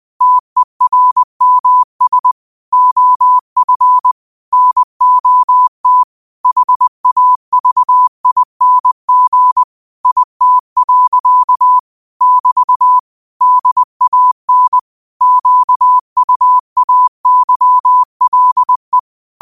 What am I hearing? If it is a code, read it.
Morse code, heard as a keyed tone, about 20 wpm, 1000 Hz. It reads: TERMS OF NOT HAVING IT. <BT> DAN QUAYLE
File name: qotdfiles / 20wpm